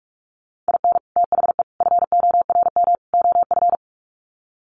Morse code, heard as a keyed tone, about 44 wpm, 700 Hz. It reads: IN THE FORM OF
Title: Sets of 4 Words 45wpm